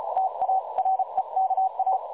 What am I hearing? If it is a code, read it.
Morse code, heard as a keyed tone, about 34 wpm, 760 Hz. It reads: TA REKA
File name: dlazaawansowanych / nagraniazpasma / websdr recording 3565.0kHz